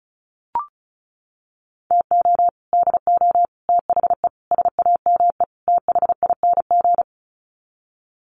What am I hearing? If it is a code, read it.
Morse code, heard as a keyed tone, about 34 wpm, 700 Hz. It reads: TO DO THE SAME THING
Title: Sets of 5 Words Encore 35wpm